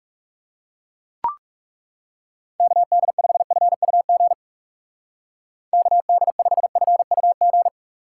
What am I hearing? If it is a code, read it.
Morse code, heard as a keyed tone, about 39 wpm, 700 Hz. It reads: KD5FUG KD5FUG